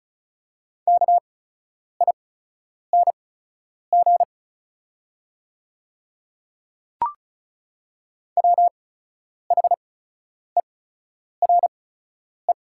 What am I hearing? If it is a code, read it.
Morse code, heard as a keyed tone, about 36 wpm, spaced as for 10 wpm, 690 Hz. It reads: KING WHERE
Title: 8x Character Spacing - Top 500 Words - 35wpm